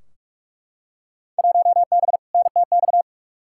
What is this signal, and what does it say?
Morse code, heard as a keyed tone, about 42 wpm, 700 Hz. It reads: 1B NTX
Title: Field Day Exchanges - 45wpm